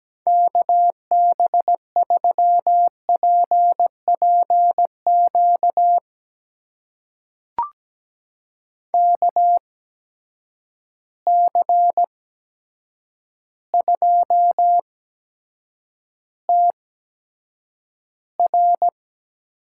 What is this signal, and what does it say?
Morse code, heard as a keyed tone, about 17 wpm, spaced as for 8 wpm, 700 Hz. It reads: KB3PPQ KC2TR
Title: USA Call Signs - 6 Characters - 8x Character Spacing - 17wpm